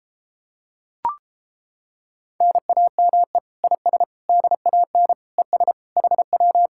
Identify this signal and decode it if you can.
Morse code, heard as a keyed tone, about 29 wpm, 700 Hz. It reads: NAME IS DAN ES HW